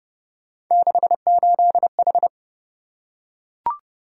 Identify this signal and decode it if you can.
Morse code, heard as a keyed tone, about 27 wpm, 700 Hz. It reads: 68H